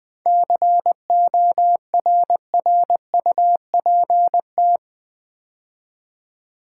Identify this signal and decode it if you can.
Morse code, heard as a keyed tone, about 20 wpm, 700 Hz. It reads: CORRUPT
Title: 7 Letter Words - First 3 Letters Given - 20wpm